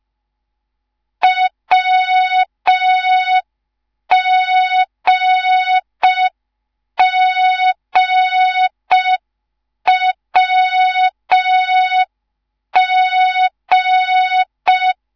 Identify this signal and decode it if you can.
Morse code, heard as a keyed tone, about 5 wpm, 3700 Hz, with noise. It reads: WGGWG